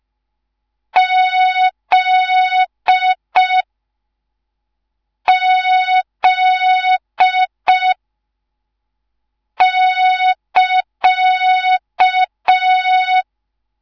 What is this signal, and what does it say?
Morse code, heard as a keyed tone, about 5 wpm, 3700 Hz. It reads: Z Z <KA>
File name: Morze-HA2MN / hang / 10GWZ HA2MN